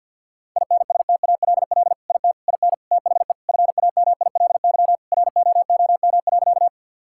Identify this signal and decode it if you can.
Morse code, heard as a keyed tone, about 50 wpm, 700 Hz. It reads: INSTALL IT IN THE FAMILY ROOM.
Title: Sentences from Top 1600 Words 50wpm